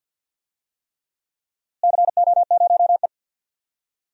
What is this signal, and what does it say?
Morse code, heard as a keyed tone, about 47 wpm, 700 Hz. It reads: CO0E
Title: DXCC-100 Call Signs - 4 Characters - Repeated 3x - 50wpm